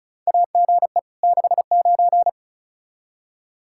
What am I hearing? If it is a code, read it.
Morse code, heard as a keyed tone, about 36 wpm, 700 Hz. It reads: AGE 69
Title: Practice QSO - Exchange 3 - More Info - 35wpm